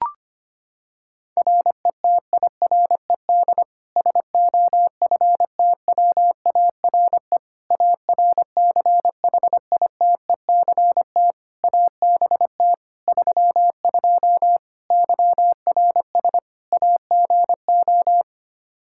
Morse code, heard as a keyed tone, about 25 wpm, 700 Hz. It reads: RETIRED SOFTWARE ARCHITECT ABT 32 YRS AGO
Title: Practice QSO - Exchange 3 - More Info - 25wpm